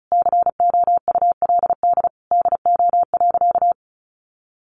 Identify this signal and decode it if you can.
Morse code, heard as a keyed tone, about 34 wpm, 700 Hz. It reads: COULD DO.
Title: Sentences from Top 1400 Words 35wpm